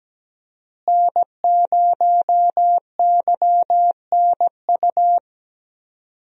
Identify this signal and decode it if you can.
Morse code, heard as a keyed tone, about 17 wpm, 700 Hz. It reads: N0YNU
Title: USA Call Signs - 5 Characters - Repeated 2x - 17wpm